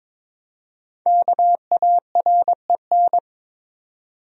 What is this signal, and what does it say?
Morse code, heard as a keyed tone, about 22 wpm, 700 Hz. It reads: KAREN
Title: USA Common Names - 22wpm